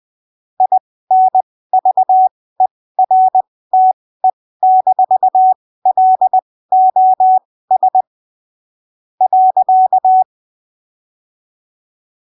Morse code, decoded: INVERTE-LOS .